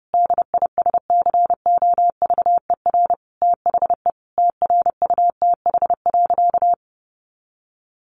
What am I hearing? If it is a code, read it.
Morse code, heard as a keyed tone, about 30 wpm, 700 Hz. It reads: DISCOVER THE TRUTH.
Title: Sentences from Top 1100 Words 30wpm